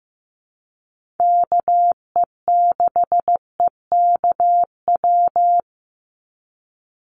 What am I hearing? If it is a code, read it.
Morse code, heard as a keyed tone, about 15 wpm, 700 Hz. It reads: KE6EKW
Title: USA Call Signs 15wpm